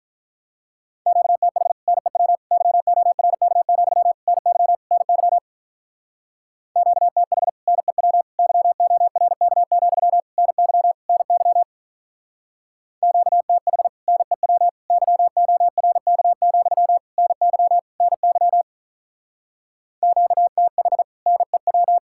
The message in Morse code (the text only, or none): QTH NEW YORK, NY NY QTH NEW YORK, NY NY QTH NEW YORK, NY NY QTH NEW